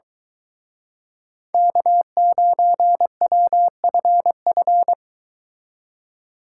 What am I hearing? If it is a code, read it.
Morse code, heard as a keyed tone, about 23 wpm, 700 Hz. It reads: K9WFF